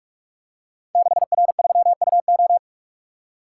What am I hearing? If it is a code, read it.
Morse code, heard as a keyed tone, about 44 wpm, 700 Hz. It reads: 6R3UO
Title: Code Groups - 5 Letter:Numbers - 45wpm